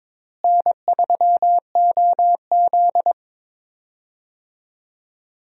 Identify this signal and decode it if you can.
Morse code, heard as a keyed tone, about 22 wpm, 700 Hz. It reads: N3OZ
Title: USA Call Signs - 4 Characters - Repeated 2x - 22wpm